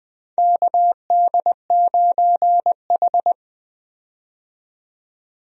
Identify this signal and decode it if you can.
Morse code, heard as a keyed tone, about 21 wpm, 700 Hz. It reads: KD9H
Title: USA Call Signs - 4 Characters - 20wpm